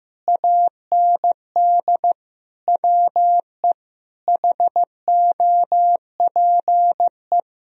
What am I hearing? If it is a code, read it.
Morse code, heard as a keyed tone, about 15 wpm, 700 Hz. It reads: AND WE HOPE
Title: Sentences from Top 1000 Words - Review Part 1 - 15wpm